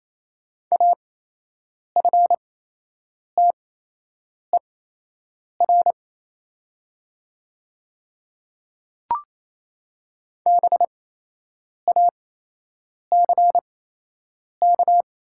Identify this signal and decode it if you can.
Morse code, heard as a keyed tone, about 28 wpm, spaced as for 8 wpm, 700 Hz. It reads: AFTER BACK
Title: 8x Character Spacing - Top 300 Words - 28wpm